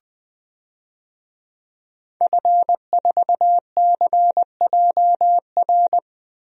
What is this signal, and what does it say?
Morse code, heard as a keyed tone, about 20 wpm, 700 Hz. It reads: F4CJR
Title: DXCC-100 Call Signs - 5 Characters - 20wpm